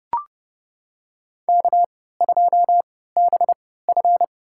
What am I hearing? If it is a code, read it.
Morse code, heard as a keyed tone, about 30 wpm, spaced as for 17 wpm, 700 Hz. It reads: K2BF
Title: USA Call Signs - 4 Characters - 3x Character Spacing - 30wpm